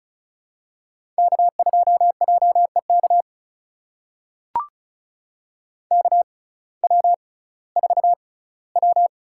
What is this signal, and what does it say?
Morse code, heard as a keyed tone, about 36 wpm, spaced as for 19 wpm, 700 Hz. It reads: K2JEK KW4W
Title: USA Call Signs - 6x Character Spacing - 35wpm